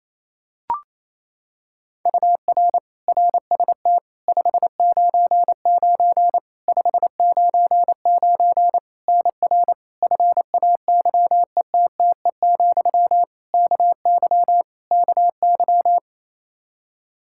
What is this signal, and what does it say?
Morse code, heard as a keyed tone, about 28 wpm, 700 Hz. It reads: UR RST 599 599 NR FAYETTE, KY KY
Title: Practice QSO - Exchange 1 - Beginning - 28wpm